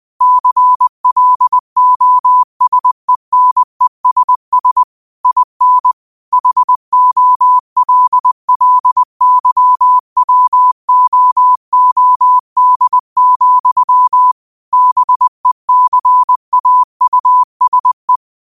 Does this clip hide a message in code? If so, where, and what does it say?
Yes: Morse code from 0.2 to 18.2 s, heard as a keyed tone, about 20 wpm, 1000 Hz. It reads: CLOSENESS IN HOLLYWOOD, BECAUSE